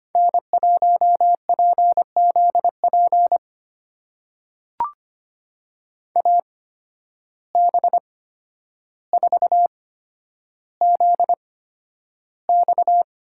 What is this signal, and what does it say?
Morse code, heard as a keyed tone, about 25 wpm, spaced as for 11 wpm, 700 Hz. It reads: N1PZP AB4ZX
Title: USA Call Signs - 5 Characters - 8x Character Spacing - 25wpm